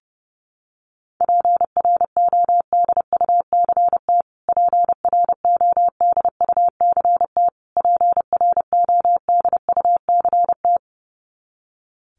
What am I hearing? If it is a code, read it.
Morse code, heard as a keyed tone, about 28 wpm, 700 Hz. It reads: PRODUCT PRODUCT PRODUCT
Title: Top 500 Words Repeated 30wpm